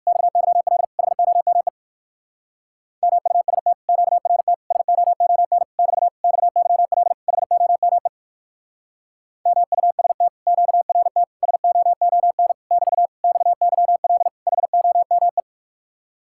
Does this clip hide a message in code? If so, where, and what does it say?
Yes: Morse code from 0.1 to 15.4 s, heard as a keyed tone, about 63 wpm, 700 Hz. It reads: XYL HOME MUST QRT SOON <BT> XYL HOME MUST QRT SOON <BT> XYL HOME